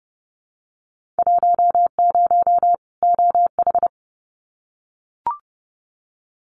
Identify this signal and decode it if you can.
Morse code, heard as a keyed tone, about 30 wpm, 700 Hz. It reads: 10 OH
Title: Winter Field Day Exchanges 30wpm